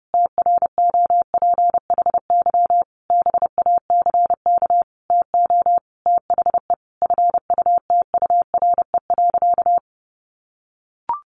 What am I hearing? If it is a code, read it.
Morse code, heard as a keyed tone, about 28 wpm, 700 Hz. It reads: TROPHY BACK TO THE FUTURE.
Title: A Sound Of Thunder 30wpm